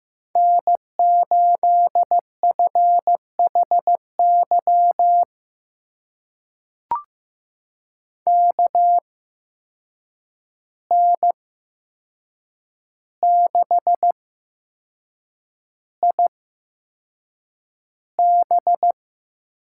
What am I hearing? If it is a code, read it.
Morse code, heard as a keyed tone, about 15 wpm, spaced as for 6 wpm, 700 Hz. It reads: N8FHY KN6IB
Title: USA Call Signs - 8x Character Spacing - 15wpm